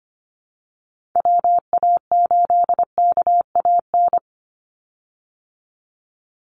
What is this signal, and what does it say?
Morse code, heard as a keyed tone, about 25 wpm, 700 Hz. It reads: WA8KAN
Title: Call Signs Repeated 3x - 25wpm